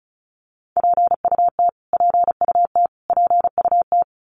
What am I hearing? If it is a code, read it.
Morse code, heard as a keyed tone, about 34 wpm, 700 Hz. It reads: PUT PUT PUT
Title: Top 300 Words Repeated 35wpm